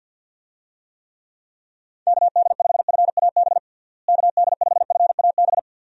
Morse code, heard as a keyed tone, about 51 wpm, 700 Hz. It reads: KD5FAB KD5FAB
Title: USA Call Signs - 6 Characters - Repeated 2x - 50wpm